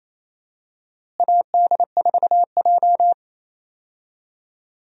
Morse code, heard as a keyed tone, about 28 wpm, 700 Hz. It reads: AD4J
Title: USA Call Signs - 4 Characters - Repeated 2x - 28wpm